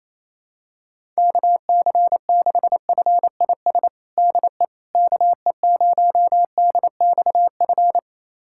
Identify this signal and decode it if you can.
Morse code, heard as a keyed tone, about 28 wpm, 700 Hz. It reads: KC6FIS DE KE0DXF
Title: USA Callsign DE Callsign - 6 Characters - 28wpm